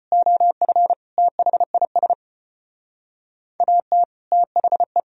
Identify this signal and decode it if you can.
Morse code, heard as a keyed tone, about 32 wpm, 700 Hz. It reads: OF THIS AT THE